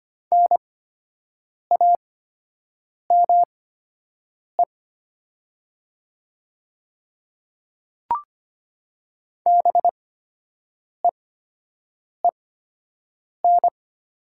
Morse code, heard as a keyed tone, about 25 wpm, spaced as for 7 wpm, 700 Hz. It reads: NAME BEEN
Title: 8x Character Spacing - Top 300 Words - 25wpm